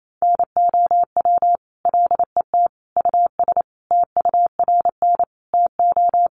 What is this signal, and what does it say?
Morse code, heard as a keyed tone, about 28 wpm, 700 Hz. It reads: NOW LET US TURN TO
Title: Sentences from Top 1400 Words 28wpm